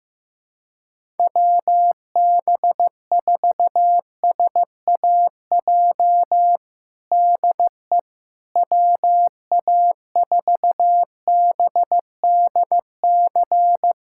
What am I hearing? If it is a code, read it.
Morse code, heard as a keyed tone, about 15 wpm, 700 Hz. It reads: WB4SAJ DE WA4BDC
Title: USA Callsign DE Callsign - 6 Characters - 15wpm